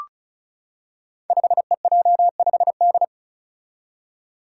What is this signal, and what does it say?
Morse code, heard as a keyed tone, about 36 wpm, 700 Hz. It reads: 5EJ5D
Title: Five Challenging Letter-Number Combinations 35wpm